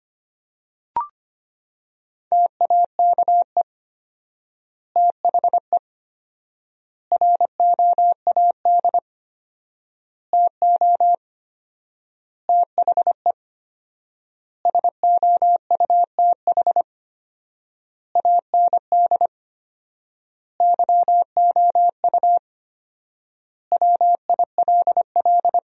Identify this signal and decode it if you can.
Morse code, heard as a keyed tone, about 25 wpm, 700 Hz. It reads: TAKE THE ROAD TO THE SOUTH AND YOU WILL
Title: 4x Word-Spacing - Sentences from Top 400 Words - 25wpm